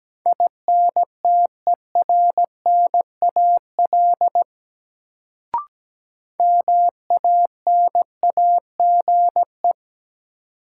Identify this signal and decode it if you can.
Morse code, heard as a keyed tone, about 17 wpm, 700 Hz. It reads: INTERNAL MANAGE
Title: Sentences from Top 1400 Words - New Word Drill Alt - 17wpm